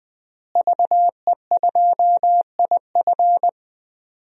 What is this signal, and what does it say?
Morse code, heard as a keyed tone, about 21 wpm, 700 Hz. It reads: VE2IF
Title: DXCC-100 Call Signs - 5 Characters - Repeated 3x - 20wpm